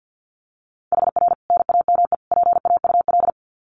Morse code, heard as a keyed tone, about 50 wpm, 700 Hz. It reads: HR NAME PAUL
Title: Common QSO Phrases 50wpm